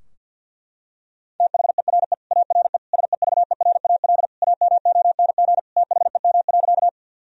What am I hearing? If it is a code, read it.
Morse code, heard as a keyed tone, about 50 wpm, 700 Hz. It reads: THERE ARE SEVERAL AMONG THEM.
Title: Sentences from Top 1000 Words - Review Part 2 - 50wpm